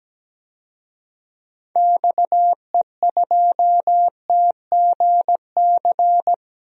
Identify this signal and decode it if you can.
Morse code, heard as a keyed tone, about 17 wpm, 700 Hz. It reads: XE2TGC
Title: DXCC-100 Call Signs - 6 Characters - Repeated 3x - 17wpm